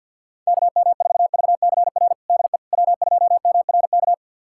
Morse code, heard as a keyed tone, about 51 wpm, 700 Hz. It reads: KM4VCR DE W2MRK
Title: USA Callsign DE Callsign 50wpm